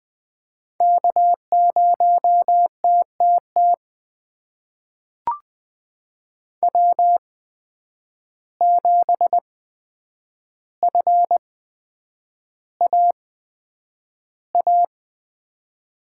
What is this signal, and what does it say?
Morse code, heard as a keyed tone, about 20 wpm, spaced as for 9 wpm, 700 Hz. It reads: K0TTT W7FAA
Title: USA Call Signs - 5 Characters - 8x Character Spacing - 20wpm